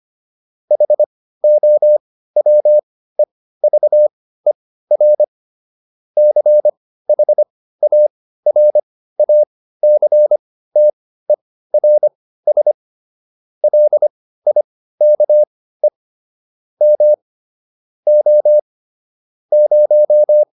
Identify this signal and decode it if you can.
Morse code, heard as a keyed tone, about 25 wpm, spaced as for 15 wpm, 600 Hz. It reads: HOWEVER CHARACTERS LIKE M O 0